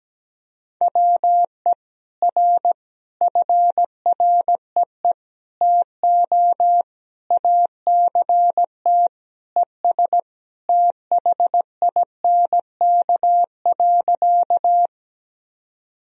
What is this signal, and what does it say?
Morse code, heard as a keyed tone, about 17 wpm, 700 Hz. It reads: WE R FREE TO ACT ES THINK.
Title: Sentences from Top 500 Words and Abbreviations 17wpm